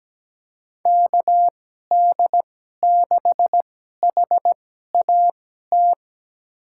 Morse code, heard as a keyed tone, about 17 wpm, spaced as for 12 wpm, 700 Hz. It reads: KD6HAT
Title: USA Call Signs - 2x Character Spacing - 17wpm